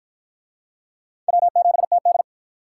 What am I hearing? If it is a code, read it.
Morse code, heard as a keyed tone, about 48 wpm, 700 Hz. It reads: W7TD